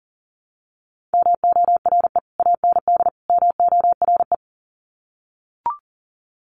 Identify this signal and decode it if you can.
Morse code, heard as a keyed tone, about 40 wpm, 700 Hz. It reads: MORE AND MORE
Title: Sets of 3 Words 40wpm